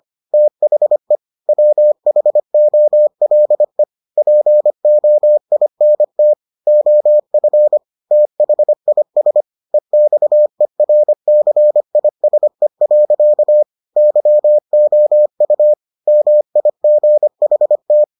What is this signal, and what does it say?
Morse code, heard as a keyed tone, about 25 wpm, 600 Hz. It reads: THE WHOLE POINT OF THIS EXERCISE. YOU MIGHT